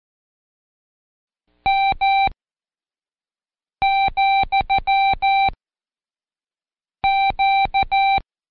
Morse code, heard as a keyed tone, about 14 wpm, 780 Hz. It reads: M , Q
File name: Code Quick / AudioDownld1 / Seg69-Day8-CheckTest 8